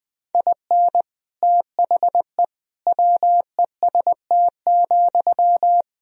Morse code, heard as a keyed tone, about 20 wpm, 700 Hz. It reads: IN THE WEST,
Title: Sentences from Top 1000 Words - Review Part 2 - 20wpm